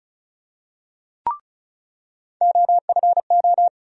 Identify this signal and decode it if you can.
Morse code, heard as a keyed tone, about 36 wpm, 700 Hz. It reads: OFO
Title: USA Call Sign Postfixes - 3 Letters 35wpm